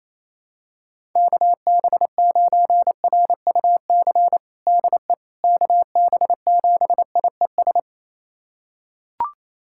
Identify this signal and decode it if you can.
Morse code, heard as a keyed tone, about 28 wpm, 700 Hz. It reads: KB9RUC DE KB7IES
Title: USA Callsign DE Callsign - 6 Characters - 28wpm